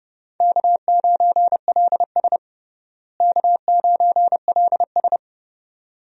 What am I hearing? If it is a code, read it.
Morse code, heard as a keyed tone, about 30 wpm, 700 Hz. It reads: K9LS K9LS
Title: USA Call Signs - 4 Characters - Repeated 2x - 30wpm